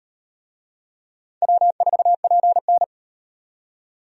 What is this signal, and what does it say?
Morse code, heard as a keyed tone, about 38 wpm, 700 Hz. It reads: W4PN